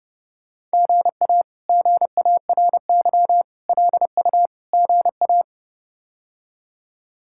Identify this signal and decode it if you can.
Morse code, heard as a keyed tone, about 32 wpm, 700 Hz. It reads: GA GARY LU GA
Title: K1USN SST QSO Practice - 30wpm